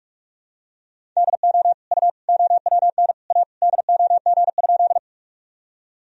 Morse code, heard as a keyed tone, about 46 wpm, 700 Hz. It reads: DO U OWN A DOG?
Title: Sentences from Top 400 Words and Abbreviations 45wpm